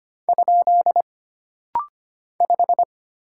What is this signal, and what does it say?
Morse code, heard as a keyed tone, about 25 wpm, 700 Hz. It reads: ? 5
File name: Single Character (Letter:Number:Punctuation:Prosign) - Rapid-Fire - 25wpm